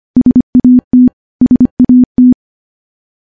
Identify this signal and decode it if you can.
Morse code, heard as a keyed tone, about 25 wpm, 250 Hz. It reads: SAT SAT
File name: morse-training-25WPM